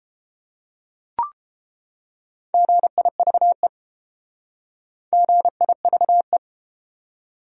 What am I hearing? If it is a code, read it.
Morse code, heard as a keyed tone, about 29 wpm, 700 Hz. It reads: GIVE GIVE